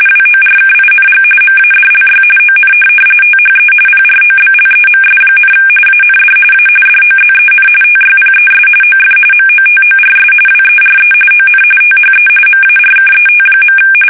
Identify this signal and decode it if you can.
RTTY, about 45.45 baud. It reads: WORKING MADLY ON THE PROBLEMS, AND A DOZEN TYPES OF MOTORS WERE ON THE MARKET.
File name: fsk850 / benjamin-g-lamme